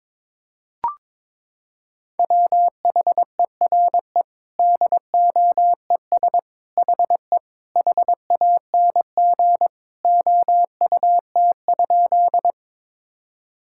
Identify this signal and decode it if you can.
Morse code, heard as a keyed tone, about 21 wpm, 700 Hz. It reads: WHERE DOES HE HANG OUT?
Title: Sentences from Top 1600 Words 22wpm